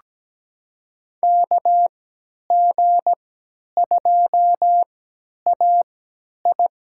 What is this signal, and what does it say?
Morse code, heard as a keyed tone, about 17 wpm, spaced as for 10 wpm, 700 Hz. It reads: KG2AI
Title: USA Call Signs - 5 Characters - 3x Character Spacing - 17wpm